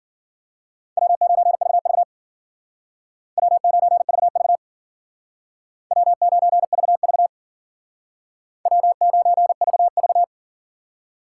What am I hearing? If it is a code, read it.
Morse code, heard as a keyed tone, about 51 wpm, 700 Hz. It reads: W9VV W9VV W9VV W9VV